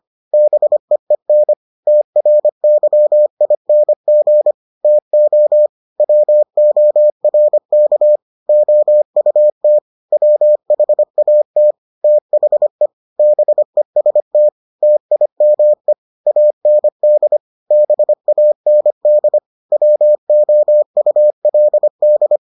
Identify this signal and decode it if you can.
Morse code, heard as a keyed tone, about 25 wpm, 600 Hz. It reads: BEEN TRYING TO WORK OUT WHAT THE BEST TIME AND BAND WOULD